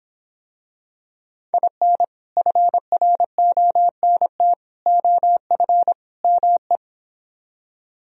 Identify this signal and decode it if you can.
Morse code, heard as a keyed tone, about 26 wpm, 700 Hz. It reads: IN FRONT OF ME